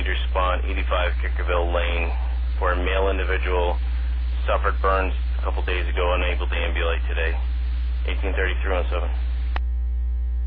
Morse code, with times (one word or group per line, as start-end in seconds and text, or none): none